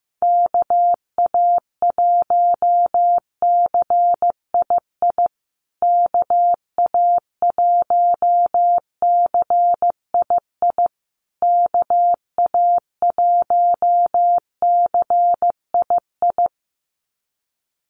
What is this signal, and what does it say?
Morse code, heard as a keyed tone, about 15 wpm, 700 Hz. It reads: KA1CII KA1CII KA1CII
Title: USA Call Signs Repeated 3x 15wpm